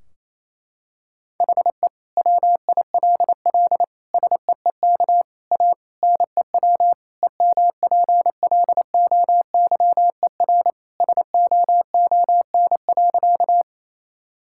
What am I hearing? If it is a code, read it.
Morse code, heard as a keyed tone, about 28 wpm, 700 Hz. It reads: HE WILL SEEK A NEW EMPLOYER SOON.
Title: Sentences from Top 1600 Words 28wpm